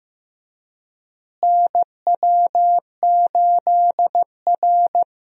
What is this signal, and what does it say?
Morse code, heard as a keyed tone, about 15 wpm, 700 Hz. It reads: NW8R
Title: USA Call Signs - 4 Characters - Repeated 2x - 15wpm